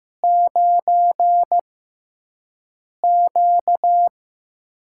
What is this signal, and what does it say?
Morse code, heard as a keyed tone, about 15 wpm, spaced as for 5 wpm, 700 Hz. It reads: 9Q
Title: USA Call Signs - 4 Characters - 6x Character Spacing - 15wpm